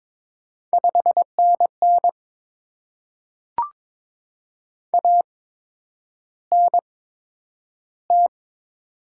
Morse code, heard as a keyed tone, about 22 wpm, spaced as for 9 wpm, 700 Hz. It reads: 5NN ANT
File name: Top 20 QSO Elements - 8x Character Spacing - 22wpm